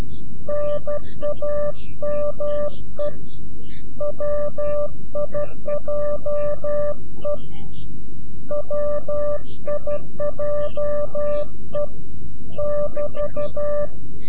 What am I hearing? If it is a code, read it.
Morse code, heard as a keyed tone, about 12 wpm, 590 Hz. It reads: NAME WIJE WIJE <BT>
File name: CW-QSO / CQ Call / CQ-QSO